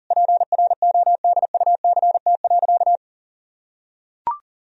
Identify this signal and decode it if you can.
Morse code, heard as a keyed tone, about 37 wpm, 700 Hz. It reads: PRODUCT.
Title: Sentences from Top 1000 Words - Review Part 3 - 40wpm